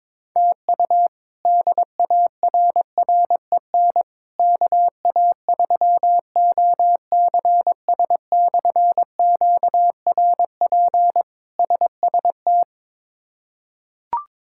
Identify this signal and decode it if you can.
Morse code, heard as a keyed tone, about 22 wpm, 700 Hz. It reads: TU DARREN KA3OCS/QRP SST
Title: K1USN SST QSO Practice - 22wpm